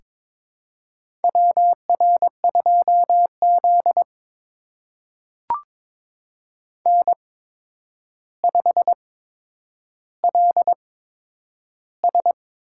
Morse code, heard as a keyed tone, about 22 wpm, spaced as for 9 wpm, 700 Hz. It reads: WR2Z N5LS
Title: USA Call Signs - 4 Characters - 8x Character Spacing - 22wpm